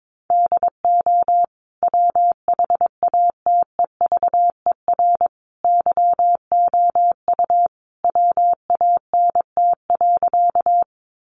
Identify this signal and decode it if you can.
Morse code, heard as a keyed tone, about 22 wpm, 700 Hz. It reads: DO WHATEVER YOU WANT.
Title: Sentences from Top 1100 Words 22wpm